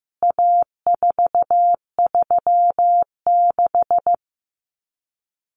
Morse code, heard as a keyed tone, about 15 wpm, 700 Hz. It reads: A436
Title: CWT Exchange 15wpm